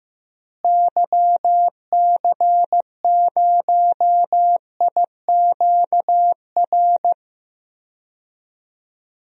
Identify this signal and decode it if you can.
Morse code, heard as a keyed tone, about 15 wpm, 700 Hz. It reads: YC0IQR
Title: DXCC-100 Call Signs - 6 Characters - Repeated 2x - 15wpm